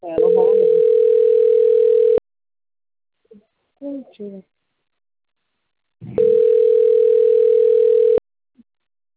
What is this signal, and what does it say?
Ringback tone (North America), 440 + 480 Hz, with speech.
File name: out- - - - - .74491